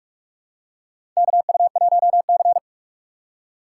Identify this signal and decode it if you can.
Morse code, heard as a keyed tone, about 43 wpm, 700 Hz. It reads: KU1C
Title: USA Call Signs - 4 Characters - 2x Character Spacing - 45wpm